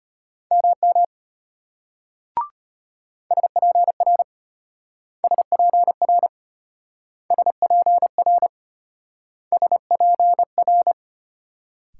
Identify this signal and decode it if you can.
Morse code, heard as a keyed tone, about 31 wpm, 700 Hz. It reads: MM SPR SPR SPR SPR
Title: Speed-Racing Multi-letter Phonemes - 25wpm